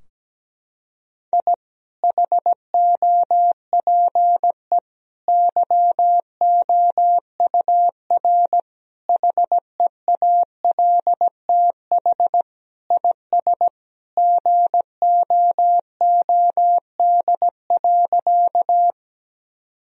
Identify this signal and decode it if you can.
Morse code, heard as a keyed tone, about 17 wpm, 700 Hz. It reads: I HOPE YOUR HEALTH IS GOOD.
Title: Sentences from Top 1000 Words - Review Part 3 - 17wpm